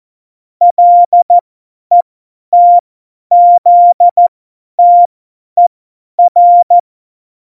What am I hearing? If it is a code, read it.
Morse code, heard as a keyed tone, about 14 wpm, spaced as for 10 wpm, 700 Hz. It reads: LETZTER
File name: Morsekurs / Witze / Witz 29. September 2020, 50 F70